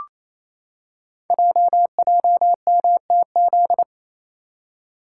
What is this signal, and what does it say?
Morse code, heard as a keyed tone, about 27 wpm, 700 Hz. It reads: JJMTZ
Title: Code Groups - 5 Letters - 28wpm